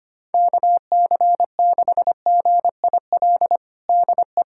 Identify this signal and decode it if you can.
Morse code, heard as a keyed tone, about 25 wpm, 700 Hz. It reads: KC6GIL DE
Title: USA Callsign DE Callsign - 6 Characters - 25wpm